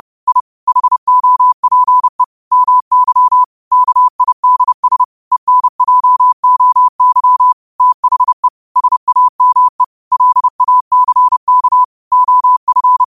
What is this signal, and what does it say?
Morse code, heard as a keyed tone, about 30 wpm, 1000 Hz. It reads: I HOPE MY KIDS ENJOY THE SAME LACK OF
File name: qotdfiles / 30wpm